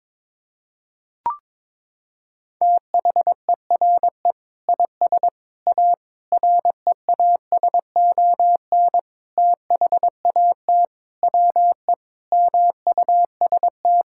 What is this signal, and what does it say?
Morse code, heard as a keyed tone, about 22 wpm, 700 Hz. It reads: THERE IS A REASON THAT WE MUST
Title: Sentences from Top 1000 Words - Review Part 5 - 22wpm